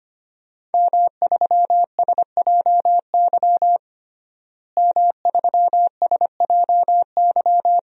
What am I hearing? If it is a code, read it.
Morse code, heard as a keyed tone, about 25 wpm, 700 Hz. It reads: M3SJY M3SJY
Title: DXCC-100 Call Signs - 5 Characters - Repeated 3x - 25wpm